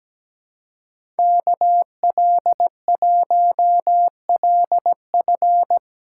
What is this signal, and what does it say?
Morse code, heard as a keyed tone, about 17 wpm, 700 Hz. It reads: KL1LF
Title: USA Callsign DE Callsign - 5 Characters - 17wpm